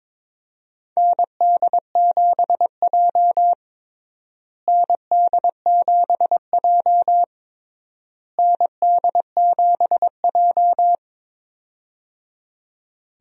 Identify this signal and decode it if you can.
Morse code, heard as a keyed tone, about 21 wpm, 700 Hz. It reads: ND7J ND7J ND7J
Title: USA Call Signs - 4 Characters - Repeated 3x - 22wpm